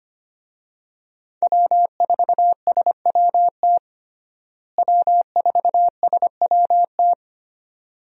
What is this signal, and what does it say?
Morse code, heard as a keyed tone, about 25 wpm, 700 Hz. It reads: W4SWT W4SWT
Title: USA Call Signs - 5 Characters - Repeated 2x - 25wpm